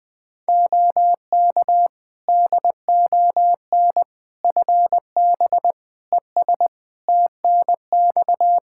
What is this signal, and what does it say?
Morse code, heard as a keyed tone, about 20 wpm, 700 Hz. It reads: OK DON FB ES TNX
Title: Practice QSO - Exchange 2 - Info - 20wpm